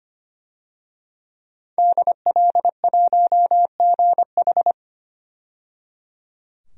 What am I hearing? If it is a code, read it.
Morse code, heard as a keyed tone, about 25 wpm, 700 Hz. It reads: DL1GH
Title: DXCC-100 Call Signs - 5 Characters - 25wpm